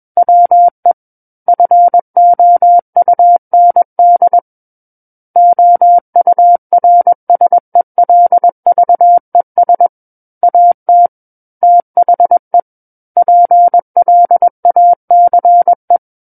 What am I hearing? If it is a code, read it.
Morse code, heard as a keyed tone, about 21 wpm, 700 Hz. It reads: WE FOUND OURSELVES AT THE PLACE
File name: Book One / 12-Chapter 12 - 21 WPM